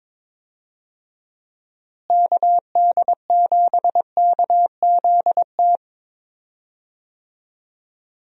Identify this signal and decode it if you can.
Morse code, heard as a keyed tone, about 21 wpm, 700 Hz. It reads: KD7KZT
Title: USA Call Signs - 6 Characters - Repeated 2x - 22wpm